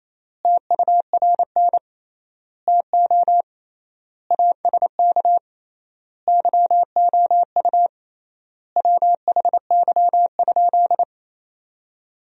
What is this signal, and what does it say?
Morse code, heard as a keyed tone, about 27 wpm, 700 Hz. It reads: TURN TO ASK YOU WHY?
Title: 3x Word-Spacing - Sentences from Top 200 Words - 28wpm